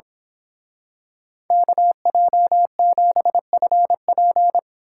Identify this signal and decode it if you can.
Morse code, heard as a keyed tone, about 25 wpm, 700 Hz. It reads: KJ7FP
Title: Speed Racing - USA 2x2 Call Signs 22wpm